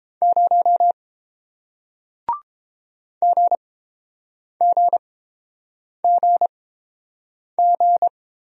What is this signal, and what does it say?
Morse code, heard as a keyed tone, about 31 wpm, spaced as for 7 wpm, 700 Hz. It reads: 0 GGGG